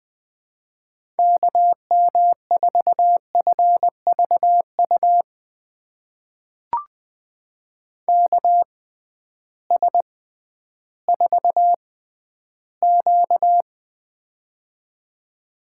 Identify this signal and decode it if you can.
Morse code, heard as a keyed tone, about 20 wpm, spaced as for 12 wpm, 700 Hz. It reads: KM4FVU KS4Q
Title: USA Call Signs - 6x Character Spacing - 20wpm